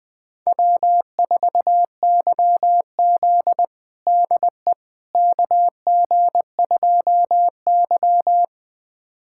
Morse code, heard as a keyed tone, about 20 wpm, 700 Hz. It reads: W4YZ DE KG2Y
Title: USA Callsign DE Callsign - 4 Characters - 20wpm